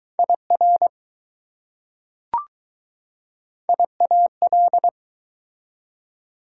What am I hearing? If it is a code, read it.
Morse code, heard as a keyed tone, about 20 wpm, 700 Hz. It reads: IR IAL